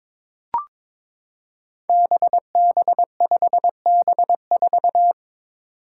Morse code, heard as a keyed tone, about 22 wpm, 700 Hz. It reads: BB5B4
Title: Five Challenging Letter-Number Combinations 22wpm